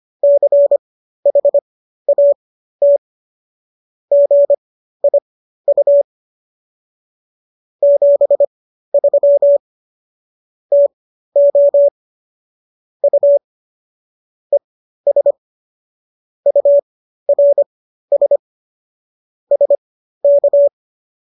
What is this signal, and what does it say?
Morse code, heard as a keyed tone, about 25 wpm, spaced as for 13 wpm, 570 Hz. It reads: CHAT GIU 73 TO U ES URS SK